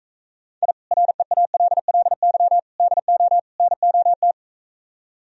Morse code, heard as a keyed tone, about 45 wpm, 700 Hz. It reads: I REALLY DO NOT